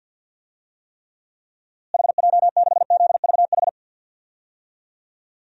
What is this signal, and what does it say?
Morse code, heard as a keyed tone, about 50 wpm, 700 Hz. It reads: HJ6ZVH
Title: DXCC-100 Call Signs - 6 Characters - Repeated 3x - 50wpm